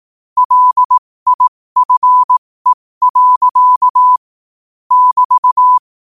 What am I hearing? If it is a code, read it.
Morse code, heard as a keyed tone, about 18 wpm, spaced as for 15 wpm, 1000 Hz. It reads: LIFE. <BT>
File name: qotdfiles / 15wpm